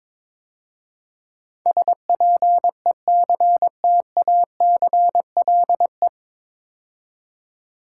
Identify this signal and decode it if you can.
Morse code, heard as a keyed tone, about 22 wpm, 700 Hz. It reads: SPECTACLE
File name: 9 Letter Words - First 3 Letters Given - 22wpm